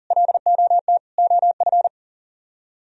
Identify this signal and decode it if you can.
Morse code, heard as a keyed tone, about 50 wpm, 700 Hz. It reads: LOT OF
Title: Speed-Racing - Sets of 4 Words 40wpm